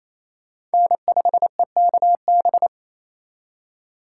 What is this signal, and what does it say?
Morse code, heard as a keyed tone, about 35 wpm, 700 Hz. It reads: N5EKB
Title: Speed Racing - USA 1x3 Call Signs 28wpm